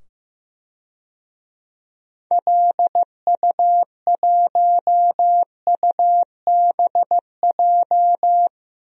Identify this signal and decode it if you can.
Morse code, heard as a keyed tone, about 15 wpm, 700 Hz. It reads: LU1UBJ
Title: DXCC-100 Call Signs - 6 Characters - 15wpm